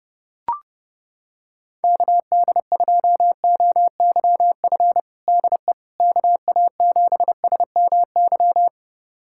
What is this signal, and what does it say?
Morse code, heard as a keyed tone, about 30 wpm, 700 Hz. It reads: KD2OYF DE KA7SMY
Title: USA Callsign DE Callsign - 6 Characters - 30wpm